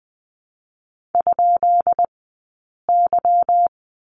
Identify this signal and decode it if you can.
Morse code, heard as a keyed tone, about 20 wpm, 700 Hz. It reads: ? Y
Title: 183 Lesson 30d - Letter b - Two Characters 20wpm